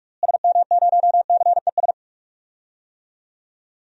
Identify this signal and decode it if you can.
Morse code, heard as a keyed tone, about 46 wpm, 700 Hz. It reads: SM0CES
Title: DXCC-100 Call Signs - 6 Characters - Repeated 2x - 45wpm